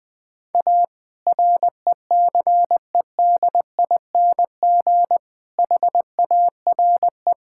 Morse code, heard as a keyed tone, about 21 wpm, 700 Hz. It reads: A RECEDING HARE
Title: All-Age Appropriate Jokes - 20wpm